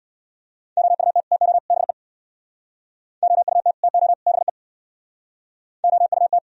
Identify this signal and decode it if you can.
Morse code, heard as a keyed tone, about 62 wpm, 700 Hz. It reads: OUT TO BE OUT TO BE OUT
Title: Speed-Racing - Sets of 3 Words 50wpm